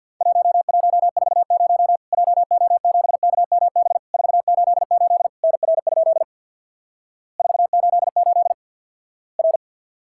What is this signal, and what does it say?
Morse code, heard as a keyed tone, about 50 wpm, 700 Hz. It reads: 1140 JO7KMB 488 NR? 488 R
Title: Worked All Europe DX Contest - QTC Exchange with Repeats - 50wpm